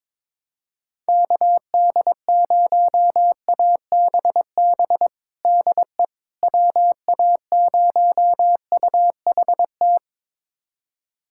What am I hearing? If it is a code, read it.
Morse code, heard as a keyed tone, about 22 wpm, 700 Hz. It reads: KD0ABB DE WA0UHT
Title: USA Callsign DE Callsign - 6 Characters - 22wpm